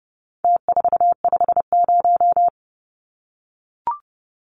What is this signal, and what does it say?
Morse code, heard as a keyed tone, about 30 wpm, 700 Hz. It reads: T450
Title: Numbers - Contest Style - 4-Digit 30wpm